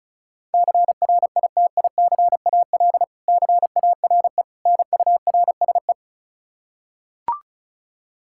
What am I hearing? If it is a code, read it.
Morse code, heard as a keyed tone, about 34 wpm, 700 Hz. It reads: CRITICAL CARE NURSE
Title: Practice QSO - Exchange 3 - More Info - 35wpm